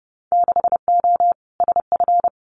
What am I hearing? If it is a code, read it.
Morse code, heard as a keyed tone, about 30 wpm, 700 Hz. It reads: 6O SF
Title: Winter Field Day Exchanges Encore 30wpm